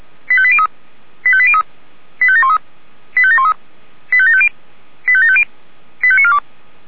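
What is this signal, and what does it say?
Five-tone selective call (ZVEI), heard as quick runs of five tones, tones of about 70 ms. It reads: 85803 85803 85713 85713 85750 85750 85942
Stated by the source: ZVEI five-tone selective call received by SDR.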